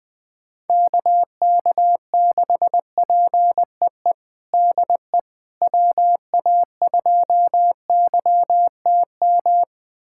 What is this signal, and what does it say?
Morse code, heard as a keyed tone, about 20 wpm, 700 Hz. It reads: KK6PEE DE WA2YTM
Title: USA Callsign DE Callsign - 6 Characters - 20wpm